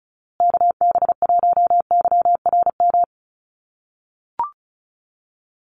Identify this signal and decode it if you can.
Morse code, heard as a keyed tone, about 34 wpm, 700 Hz. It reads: KB1YRM
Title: USA Call Signs Repeated 2x 35wpm